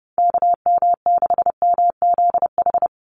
Morse code, heard as a keyed tone, about 30 wpm, 700 Hz. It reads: KM6MZH
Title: USA Call Signs 30wpm